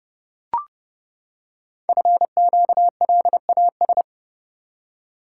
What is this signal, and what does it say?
Morse code, heard as a keyed tone, about 29 wpm, 700 Hz. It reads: FQLAS
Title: Code Groups - 5 Letters - 30wpm